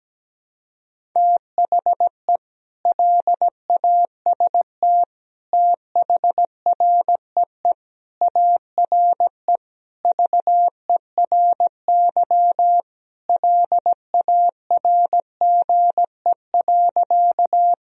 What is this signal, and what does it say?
Morse code, heard as a keyed tone, about 17 wpm, 700 Hz. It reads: THE LAST THREE ARE VERY LARGE.
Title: Sentences from Top 1000 Words - Review Part 1 - 17wpm